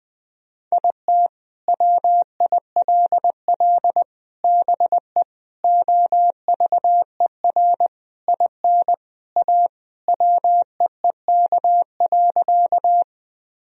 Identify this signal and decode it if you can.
Morse code, heard as a keyed tone, about 20 wpm, 700 Hz. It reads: IT WILL BE OVER IN A WEEK.
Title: Sentences from Top 1000 Words - Review Part 2 - 20wpm